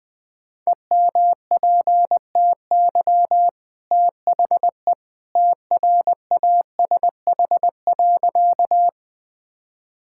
Morse code, heard as a keyed tone, about 20 wpm, 700 Hz. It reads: EMPTY THE TRASH.
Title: Easy Sentences with Hints - 20wpm